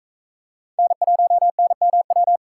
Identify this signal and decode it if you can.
Morse code, heard as a keyed tone, about 43 wpm, 700 Hz. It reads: N1NMW